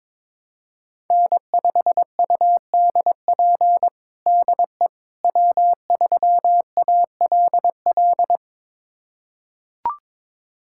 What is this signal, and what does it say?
Morse code, heard as a keyed tone, about 22 wpm, 700 Hz. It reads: N5UDP DE W3ALL
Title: USA Callsign DE Callsign - 5 Characters - 22wpm